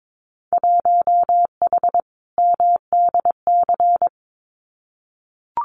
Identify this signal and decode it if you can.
Morse code, heard as a keyed tone, about 22 wpm, 700 Hz. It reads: 1H MDC
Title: Winter Field Day Exchanges 22wpm